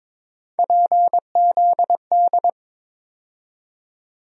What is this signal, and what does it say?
Morse code, heard as a keyed tone, about 22 wpm, 700 Hz. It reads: PZD
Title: USA Call Sign Postfixes - 3 Letters 22wpm